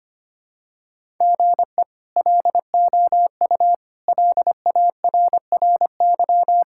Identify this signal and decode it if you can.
Morse code, heard as a keyed tone, about 25 wpm, 700 Hz. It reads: GE LOU LARRY
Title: K1USN SST QSO Practice - 25wpm